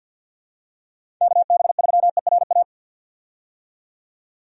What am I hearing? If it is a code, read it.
Morse code, heard as a keyed tone, about 53 wpm, 700 Hz. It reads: KB3ERA
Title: USA Call Signs - 6 Characters - 50wpm